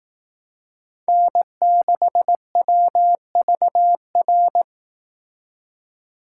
Morse code, heard as a keyed tone, about 18 wpm, 700 Hz. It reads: N6WVR
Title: Speed Racing - USA Call Signs 15wpm